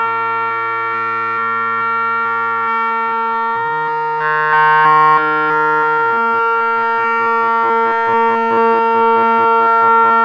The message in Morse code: none